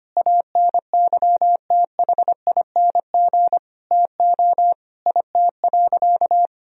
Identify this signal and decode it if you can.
Morse code, heard as a keyed tone, about 25 wpm, 700 Hz. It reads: ANYTHING TO IT.
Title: Sentences from Top 1000 Words - Review Part 1 - 25wpm